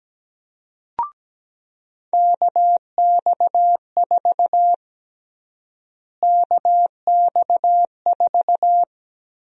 Morse code, heard as a keyed tone, about 17 wpm, 700 Hz. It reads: KX4 KX4
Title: USA Call Sign Prefixes - Repeated 2x - 17wpm